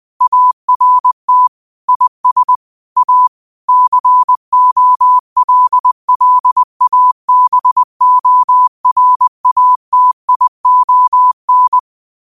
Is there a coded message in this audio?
Yes: Morse code from 0.2 to 11.8 s, heard as a keyed tone, about 20 wpm, 1000 Hz. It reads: ART IS A COLLABORATION